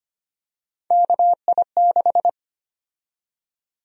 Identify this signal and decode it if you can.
Morse code, heard as a keyed tone, about 25 wpm, 700 Hz. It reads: KI6
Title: USA Call Sign Prefixes - Repeated 2x - 25wpm